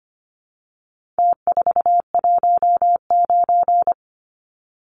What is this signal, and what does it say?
Morse code, heard as a keyed tone, about 25 wpm, 700 Hz. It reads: T419
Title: Numbers - Contest Style - 4-Digit 25wpm